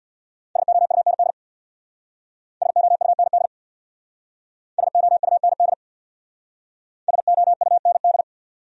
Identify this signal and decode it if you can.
Morse code, heard as a keyed tone, about 67 wpm, 690 Hz. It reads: SOUND SOUND SOUND SOUND